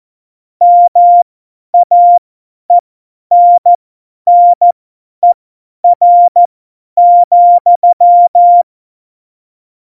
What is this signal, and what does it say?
Morse code, heard as a keyed tone, about 14 wpm, spaced as for 10 wpm, 700 Hz. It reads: MAENNER,
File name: Morsekurs / Witze / Witz 11 August 2020, 50 F70